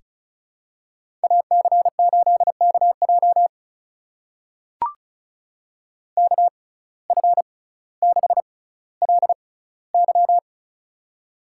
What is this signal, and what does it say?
Morse code, heard as a keyed tone, about 36 wpm, spaced as for 18 wpm, 700 Hz. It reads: AC8KJ KF6LY
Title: USA Call Signs - 5 Characters - 6x Character Spacing - 35wpm